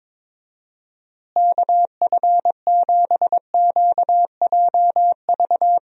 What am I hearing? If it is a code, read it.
Morse code, heard as a keyed tone, about 22 wpm, 700 Hz. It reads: KF7QJV
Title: USA Call Signs - 6 Characters - Repeated 3x - 22wpm